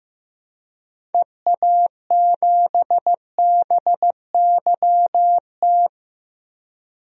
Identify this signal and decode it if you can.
Morse code, heard as a keyed tone, about 15 wpm, 700 Hz. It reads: EA7BYT
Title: DXCC-100 Call Signs - 6 Characters - 15wpm